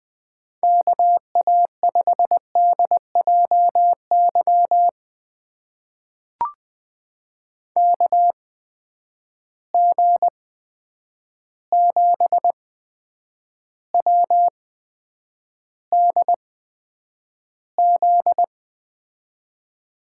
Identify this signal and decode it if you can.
Morse code, heard as a keyed tone, about 20 wpm, spaced as for 9 wpm, 700 Hz. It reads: KA5DJY KG7WDZ
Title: USA Call Signs - 6 Characters - 8x Character Spacing - 20wpm